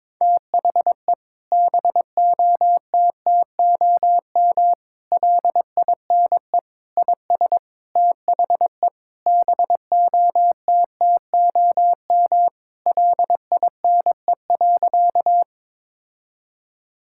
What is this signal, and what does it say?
Morse code, heard as a keyed tone, about 22 wpm, 700 Hz. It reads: THE BOTTOM LINE IS THE BOTTOM LINE.
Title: Sentences from Top 1000 Words - Review Part 6 - 22wpm